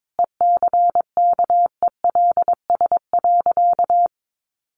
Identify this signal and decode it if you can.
Morse code, heard as a keyed tone, about 22 wpm, 700 Hz. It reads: ECKELS.
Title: A Sound Of Thunder 22wpm